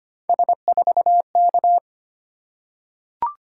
Morse code, heard as a keyed tone, about 25 wpm, 700 Hz. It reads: S4K
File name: Three Letter-Number Combinations - 25wpm